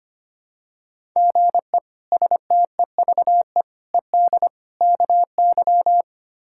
Morse code, heard as a keyed tone, about 25 wpm, 700 Hz. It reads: GE STEVE ED KY
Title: K1USN SST QSO Practice - 25wpm